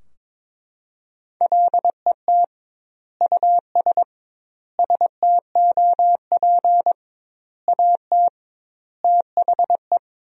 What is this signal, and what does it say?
Morse code, heard as a keyed tone, about 22 wpm, 700 Hz. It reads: LET US STOP AT THE
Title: 2x Word-Spacing - Sentences from Top 400 Words - 22wpm